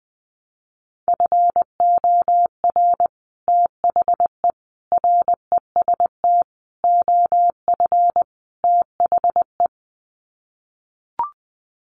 Morse code, heard as a keyed tone, about 19 wpm, 700 Hz. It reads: FOR THE REST OF THE
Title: Sets of 5 Words 20wpm